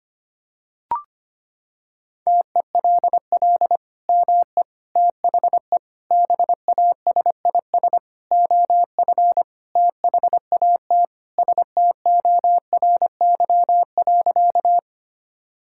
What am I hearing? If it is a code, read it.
Morse code, heard as a keyed tone, about 25 wpm, 700 Hz. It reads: TELL ME THE BASIS OF THAT STORY.
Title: Sentences from Top 1000 Words - Review Part 5 - 25wpm